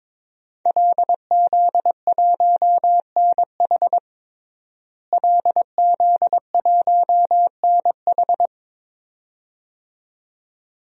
Morse code, heard as a keyed tone, about 22 wpm, 700 Hz. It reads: LZ1NH LZ1NH
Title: DXCC-100 Call Signs - 5 Characters - Repeated 2x - 22wpm